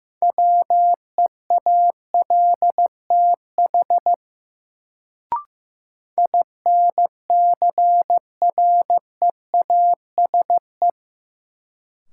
Morse code, heard as a keyed tone, about 15 wpm, 700 Hz. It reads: WEALTH INCREASE
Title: Sentences from Top 1000 Words - Review Part 4 - New Word Drill - 15wpm